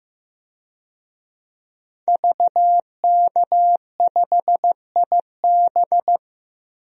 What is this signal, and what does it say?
Morse code, heard as a keyed tone, about 15 wpm, 700 Hz. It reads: VK5IB
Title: DXCC-100 Call Signs - 5 Characters - 15wpm